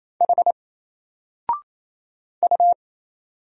Morse code, heard as a keyed tone, about 27 wpm, 700 Hz. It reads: H U
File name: Single-Letter-Number-28wpm